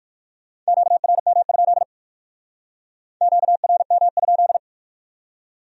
Morse code, heard as a keyed tone, about 57 wpm, 700 Hz. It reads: QRM? QRM?